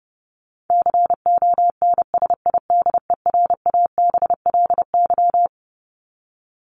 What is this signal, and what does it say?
Morse code, heard as a keyed tone, about 30 wpm, 700 Hz. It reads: CONSIDERABLY
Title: LY Suffix - Top 100 Words with Base Words 30wpm